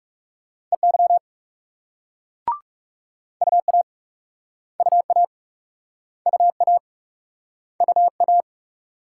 Morse code, heard as a keyed tone, about 38 wpm, 700 Hz. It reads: EY UA UA UA UA